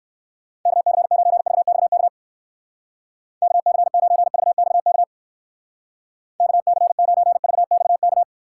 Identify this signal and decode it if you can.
Morse code, heard as a keyed tone, about 56 wpm, 700 Hz. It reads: KC9VXK KC9VXK KC9VXK
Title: Speed Racing - USA Call Signs 45wpm